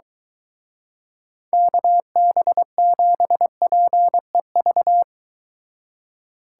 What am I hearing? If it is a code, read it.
Morse code, heard as a keyed tone, about 23 wpm, 700 Hz. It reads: KB7PEV